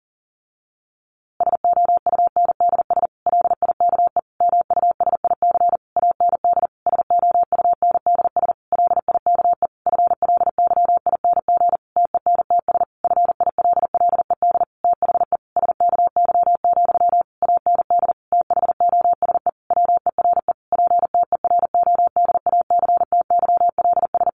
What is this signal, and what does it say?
Morse code, heard as a keyed tone, about 40 wpm, 700 Hz. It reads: SOUNDS LIKE MUSIC AND SOUNDS LIKE FLYING TENTS FILLED THE SKY, AND THOSE WERE PTERODACTYLS
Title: A Sound Of Thunder 40wpm